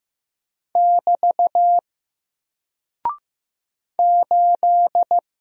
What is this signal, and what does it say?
Morse code, heard as a keyed tone, about 15 wpm, 700 Hz. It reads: <BT> 8
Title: Single Character (Letter:Number:Punctuation:Prosign) 15wpm